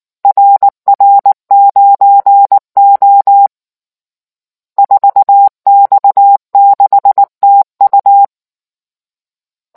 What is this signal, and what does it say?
Morse code, heard as a keyed tone, about 19 wpm, 800 Hz. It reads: RR9O 4X6TU